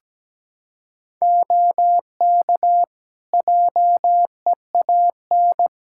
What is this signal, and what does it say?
Morse code, heard as a keyed tone, about 17 wpm, 700 Hz. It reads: OK JEAN
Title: Practice QSO - Exchange 2 - Info - 17wpm